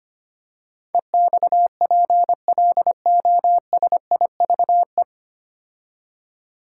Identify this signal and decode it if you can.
Morse code, heard as a keyed tone, about 25 wpm, 690 Hz. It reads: EXPLOSIVE
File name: practice / 25wpm / 9 Letter Words - First 3 Letters Given - 25wpm